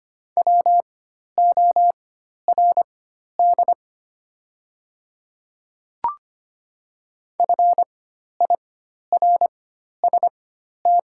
Morse code, heard as a keyed tone, about 25 wpm, spaced as for 12 wpm, 700 Hz. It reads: WORD FIRST
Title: 4x Character Spacing - Top 100 Words - 25wpm